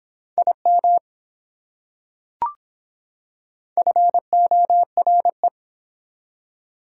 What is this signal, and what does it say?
Morse code, heard as a keyed tone, about 24 wpm, 700 Hz. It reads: IM FORE